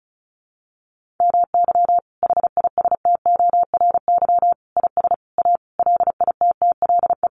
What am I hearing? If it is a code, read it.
Morse code, heard as a keyed tone, about 34 wpm, 700 Hz. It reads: MY HISTORY IS A LITTLE
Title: Sentences from Top 1500 Words 35wpm